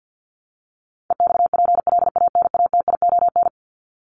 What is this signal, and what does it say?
Morse code, heard as a keyed tone, about 50 wpm, 700 Hz. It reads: EXPLANATION
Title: Top 1751-2000 Words 50wpm